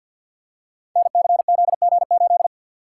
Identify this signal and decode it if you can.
Morse code, heard as a keyed tone, about 48 wpm, 700 Hz. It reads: NCZG8
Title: Code Groups - 5 Letter:Numbers - 50wpm